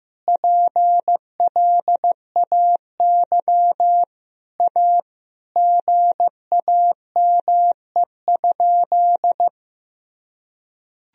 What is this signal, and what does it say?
Morse code, heard as a keyed tone, about 15 wpm, 700 Hz. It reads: PLAY A GAME?
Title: Sentences from Top 1000 Words - Review Part 1 - 15wpm